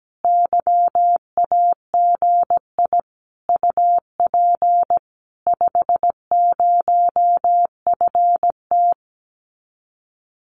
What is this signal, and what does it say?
Morse code, heard as a keyed tone, about 17 wpm, 700 Hz. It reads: YAGI UP 50FT
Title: Common QSO Phrases Encore 17wpm